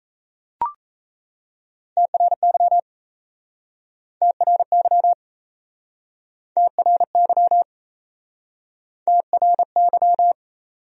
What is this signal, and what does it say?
Morse code, heard as a keyed tone, about 37 wpm, 700 Hz. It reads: TRY TRY TRY TRY